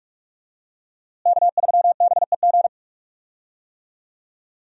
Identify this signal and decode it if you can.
Morse code, heard as a keyed tone, about 46 wpm, 700 Hz. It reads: K3BEG
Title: USA Call Signs - 5 Characters - Repeated 2x - 45wpm